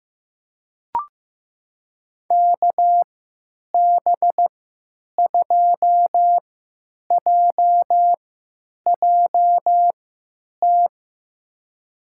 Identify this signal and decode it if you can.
Morse code, heard as a keyed tone, about 15 wpm, spaced as for 9 wpm, 700 Hz. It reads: KB2JJT
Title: USA Call Signs - 6 Characters - 3x Character Spacing - 15wpm